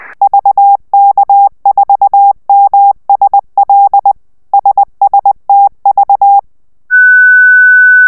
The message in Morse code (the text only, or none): VK4MSL SSTV